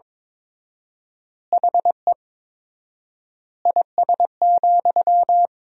Morse code, heard as a keyed tone, about 21 wpm, 700 Hz. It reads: HE IS,
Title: 4x Word-Spacing - Sentences from Top 100 Words - 22wpm